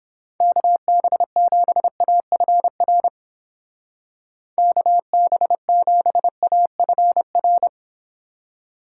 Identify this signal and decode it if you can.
Morse code, heard as a keyed tone, about 29 wpm, 700 Hz. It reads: KB7AFR KB7AFR